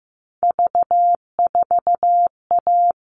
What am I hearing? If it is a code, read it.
Morse code, heard as a keyed tone, about 15 wpm, 700 Hz. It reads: V4A
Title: All Cut Numbers - 3-Digit 15wpm